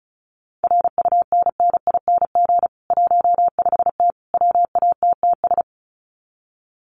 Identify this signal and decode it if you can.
Morse code, heard as a keyed tone, about 34 wpm, 700 Hz. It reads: RUNNING 15T WATTS
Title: Common QSO Phrases 35wpm